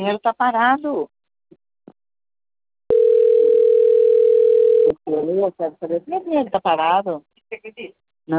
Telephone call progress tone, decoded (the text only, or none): ringback tone